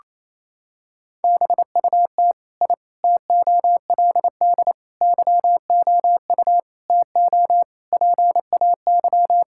Morse code, heard as a keyed tone, about 28 wpm, 700 Hz. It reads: BUT I TOLD YOU TO PAY
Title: Sentences from Top 1000 Words - Review Part 6 - 28wpm